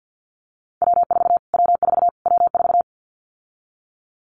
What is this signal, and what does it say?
Morse code, heard as a keyed tone, about 49 wpm, 700 Hz. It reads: W4 W4 W4
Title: Callsign Prefixes for Top 25 DXCC - Repeated 3x - 50wpm